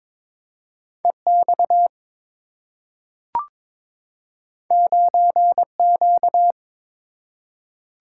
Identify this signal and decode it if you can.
Morse code, heard as a keyed tone, about 22 wpm, 700 Hz. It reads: EX 9Q
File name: practice / 22wpm / Two Letter-Number Combinations - 22wpm